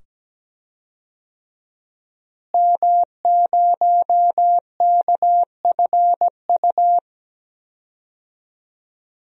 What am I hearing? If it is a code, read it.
Morse code, heard as a keyed tone, about 17 wpm, 700 Hz. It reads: M0KFU
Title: DXCC-100 Call Signs - 5 Characters - Repeated 2x - 17wpm